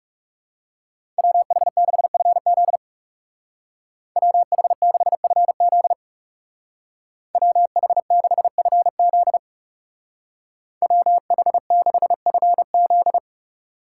Morse code, heard as a keyed tone, about 37 wpm, 700 Hz. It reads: WH6FZ WH6FZ WH6FZ WH6FZ
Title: Speed Racing - USA 2x2 Call Signs 30wpm